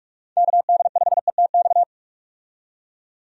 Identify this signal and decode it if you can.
Morse code, heard as a keyed tone, about 48 wpm, 700 Hz. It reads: KD5ETX
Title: USA Call Signs - 6 Characters - 45wpm